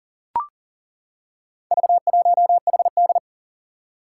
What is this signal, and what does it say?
Morse code, heard as a keyed tone, about 37 wpm, 700 Hz. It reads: V1HD
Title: Four Challenging Letter-Number Combinations 40wpm